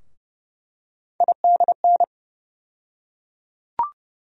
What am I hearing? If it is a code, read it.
Morse code, heard as a keyed tone, about 30 wpm, 700 Hz. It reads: IDN
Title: USA Call Sign Postfixes - 3 Letters 30wpm